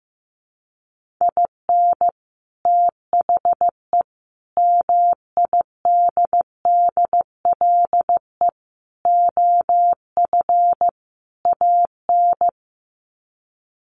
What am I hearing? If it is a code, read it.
Morse code, heard as a keyed tone, about 15 wpm, 700 Hz. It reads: IN THE MIDDLE OF AN
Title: Sets of 5 Words Encore2 15wpm